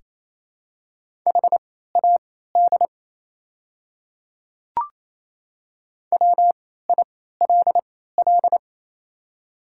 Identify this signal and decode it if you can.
Morse code, heard as a keyed tone, about 28 wpm, spaced as for 16 wpm, 690 Hz. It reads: HAD WILL
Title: 3x Character Spacing - Top 100 Words - 28wpm